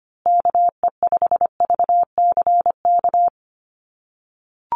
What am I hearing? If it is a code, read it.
Morse code, heard as a keyed tone, about 25 wpm, 700 Hz. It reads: KE5VCK
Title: USA Call Signs Repeated 2x 25wpm